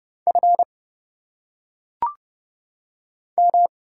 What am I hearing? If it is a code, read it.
Morse code, heard as a keyed tone, about 31 wpm, 700 Hz. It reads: F M
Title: Single Letters - Phonetic Pronunciation 30wpm